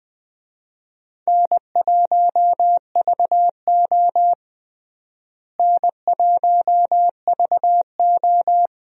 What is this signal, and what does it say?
Morse code, heard as a keyed tone, about 20 wpm, 700 Hz. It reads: N1VO N1VO
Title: USA Call Signs - 4 Characters - Repeated 2x - 20wpm